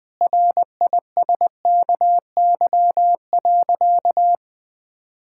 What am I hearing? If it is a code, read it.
Morse code, heard as a keyed tone, about 20 wpm, 700 Hz. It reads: RISKY.
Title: Sentences with 6-Word Phrases - 20wpm